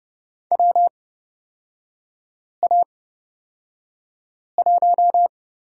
Morse code, heard as a keyed tone, about 31 wpm, spaced as for 5 wpm, 700 Hz. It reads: WA1
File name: Lesson 36e - BT - Three Characters 30wpm